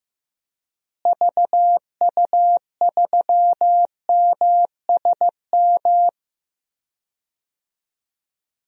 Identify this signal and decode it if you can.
Morse code, heard as a keyed tone, about 15 wpm, 700 Hz. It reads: VU3MSM
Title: DXCC-100 Call Signs - 6 Characters - Repeated 3x - 15wpm